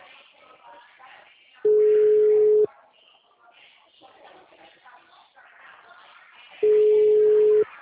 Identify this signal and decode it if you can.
Ringback tone (Europe, 425 Hz), with speech.